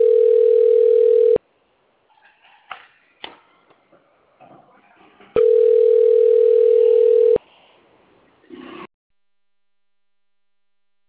Ringback tone (North America), 440 + 480 Hz.